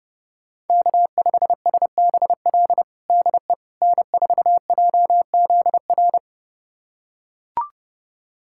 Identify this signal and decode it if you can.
Morse code, heard as a keyed tone, about 32 wpm, 700 Hz. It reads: K5SBL DE N4JZR
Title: USA Callsign DE Callsign - 5 Characters - 30wpm